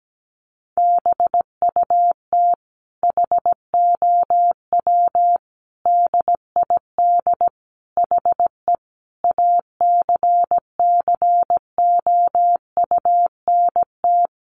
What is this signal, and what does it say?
Morse code, heard as a keyed tone, about 17 wpm, 700 Hz. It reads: BUT HOW DID HE ACCOUNT
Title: Sentences from Top 900 Words 17wpm